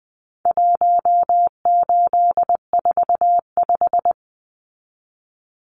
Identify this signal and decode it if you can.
Morse code, heard as a keyed tone, about 20 wpm, 700 Hz. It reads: 1845
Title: Numbers - 4-Digit 20wpm